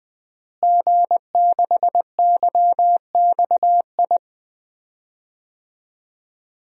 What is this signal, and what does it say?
Morse code, heard as a keyed tone, about 20 wpm, 700 Hz. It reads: G6YXI
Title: DXCC-100 Call Signs - 5 Characters - Repeated 3x - 20wpm